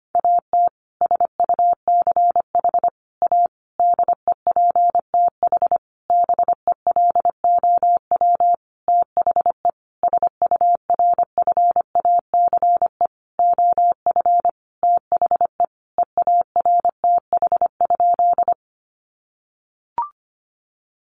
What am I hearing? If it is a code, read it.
Morse code, heard as a keyed tone, about 25 wpm, 700 Hz. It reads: AT SUCH A DEPTH BELOW THE SURFACE OF THE EARTH?
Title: Sentences from Top 1200 Words 25wpm